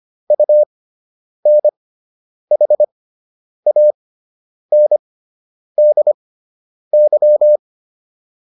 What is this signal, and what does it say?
Morse code, heard as a keyed tone, about 25 wpm, spaced as for 9 wpm, 600 Hz. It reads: UNHANDY